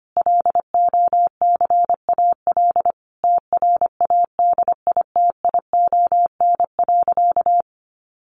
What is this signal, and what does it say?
Morse code, heard as a keyed tone, about 25 wpm, 700 Hz. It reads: LOCAL TRADITION.
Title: Sentences from Top 1500 Words 25wpm